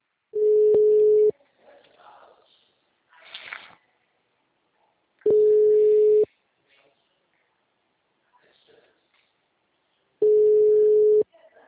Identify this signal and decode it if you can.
Ringback tone (Europe, 425 Hz), with speech.